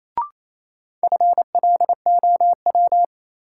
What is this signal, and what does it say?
Morse code, heard as a keyed tone, about 28 wpm, 700 Hz. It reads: FLOW
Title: Sentences from Top 1200 Words - New Word Drill - 28wpm